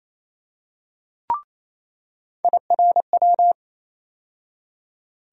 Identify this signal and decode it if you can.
Morse code, heard as a keyed tone, about 28 wpm, 700 Hz. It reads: IRW
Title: USA Call Sign Postfixes - 3 Letters 28wpm